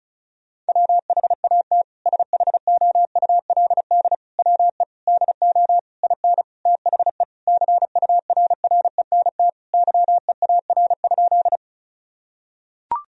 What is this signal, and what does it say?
Morse code, heard as a keyed tone, about 34 wpm, 700 Hz. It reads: WHAT SHOULD WE DO IN THE CURRENT YEAR?
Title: Sentences from Top 1000 Words - Review Part 4 - 35wpm